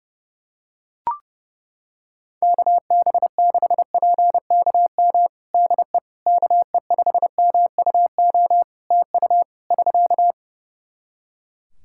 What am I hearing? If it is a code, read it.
Morse code, heard as a keyed tone, about 30 wpm, 700 Hz. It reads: KB6PKM DE KE5MUO TU <SK>
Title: Practice QSO - Exchange 4 - Ending - 30wpm